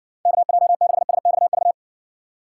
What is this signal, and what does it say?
Morse code, heard as a keyed tone, about 60 wpm, 700 Hz. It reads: KJ6SXV